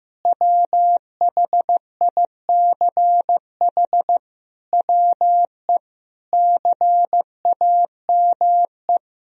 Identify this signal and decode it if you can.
Morse code, heard as a keyed tone, about 16 wpm, 700 Hz. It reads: WHICH WE CAME
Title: Sentences from Top 1000 Words - Review Part 4 - 15wpm